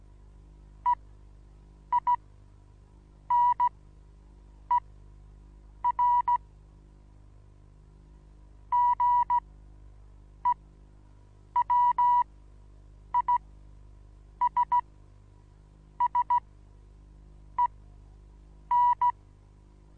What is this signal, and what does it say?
Morse code, heard as a keyed tone, about 15 wpm, spaced as for 6 wpm, 970 Hz. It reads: EINER GEWISSEN
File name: Onlinelehrgang / morsen / DJ4UF Morsen25